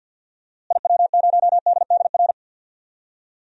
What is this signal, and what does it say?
Morse code, heard as a keyed tone, about 47 wpm, 700 Hz. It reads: IW0DDR
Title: DXCC-100 Call Signs - 6 Characters - Repeated 3x - 50wpm